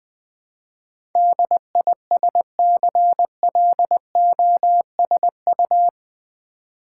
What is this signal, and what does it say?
Morse code, heard as a keyed tone, about 21 wpm, 700 Hz. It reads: DISCLOSU
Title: Build A Word - 4 Letters At A Time - 10-Letter Words - 20wpm